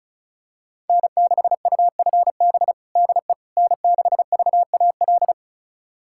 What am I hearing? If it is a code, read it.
Morse code, heard as a keyed tone, about 36 wpm, 700 Hz. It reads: N6UFB DE N6VAL
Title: USA Callsign DE Callsign - 5 Characters - 35wpm